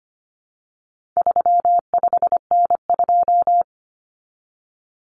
Morse code, heard as a keyed tone, about 25 wpm, 700 Hz. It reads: 35N2
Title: Common Cut Numbers - 4-Digit 25wpm